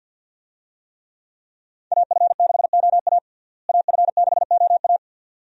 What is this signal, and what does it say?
Morse code, heard as a keyed tone, about 46 wpm, 700 Hz. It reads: AF6OA AF6OA
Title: USA Call Signs - 5 Characters - Repeated 2x - 50wpm